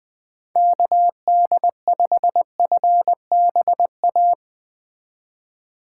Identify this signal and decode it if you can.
Morse code, heard as a keyed tone, about 21 wpm, 700 Hz. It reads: KD5FBA
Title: K1USN SST QSO Practice - 20wpm